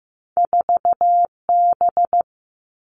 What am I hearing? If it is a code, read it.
Morse code, heard as a keyed tone, about 15 wpm, 700 Hz. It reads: 4B
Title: All Cut Numbers - 2-Digit 15wpm